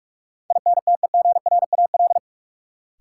Morse code, heard as a keyed tone, about 46 wpm, 700 Hz. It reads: INTEGRAL
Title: Sets of 5 Words - New Word Drill - 45wpm